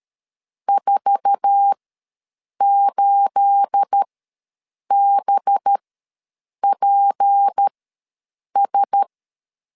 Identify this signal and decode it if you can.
Morse code, heard as a keyed tone, about 13 wpm, spaced as for 7 wpm, 800 Hz. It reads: 48BPS
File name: LEZIONI / 9 28 ws 12wpm 3354